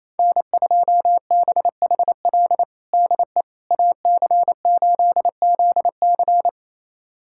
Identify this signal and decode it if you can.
Morse code, heard as a keyed tone, about 27 wpm, 700 Hz. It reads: N2BHL DE AC8ZC
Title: USA Callsign DE Callsign - 5 Characters - 28wpm